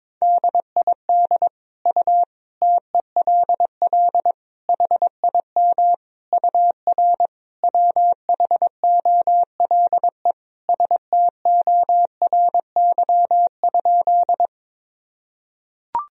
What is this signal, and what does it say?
Morse code, heard as a keyed tone, about 22 wpm, 700 Hz. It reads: DID U TELL HIM UR WHOLE STORY?
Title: Sentences from Top 400 Words and Abbreviations 22wpm